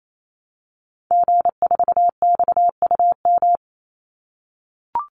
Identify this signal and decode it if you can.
Morse code, heard as a keyed tone, about 28 wpm, 700 Hz. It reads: G4XUM
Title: Call Signs - 28wpm